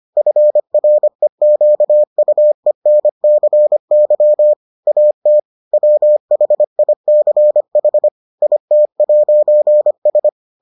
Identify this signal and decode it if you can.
Morse code, heard as a keyed tone, about 25 wpm, 600 Hz. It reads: FREQUENCY AT WHICH IT'S